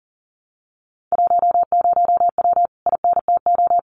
Morse code, heard as a keyed tone, about 40 wpm, 700 Hz. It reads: 10W INTO
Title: Common QSO Phrases Encore 40wpm